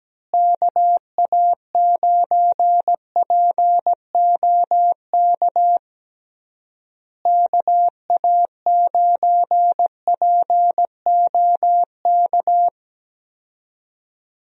Morse code, heard as a keyed tone, about 17 wpm, 700 Hz. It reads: KA9POK KA9POK
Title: USA Call Signs - 6 Characters - Repeated 3x - 17wpm